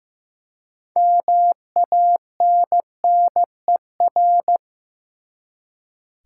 Morse code, heard as a keyed tone, about 15 wpm, 700 Hz. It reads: MANNER
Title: Six Letter Words - 15wpm